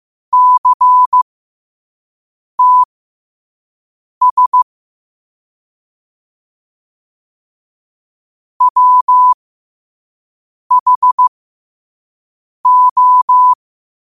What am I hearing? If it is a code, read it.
Morse code, heard as a keyed tone, about 15 wpm, spaced as for 5 wpm, 1000 Hz. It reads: CTS WHO
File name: qotdfiles / 5wpm